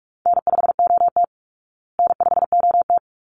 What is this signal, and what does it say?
Morse code, heard as a keyed tone, about 44 wpm, 700 Hz. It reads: N5OT N5OT
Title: Easy Call Signs Repeated 2x - 45wpm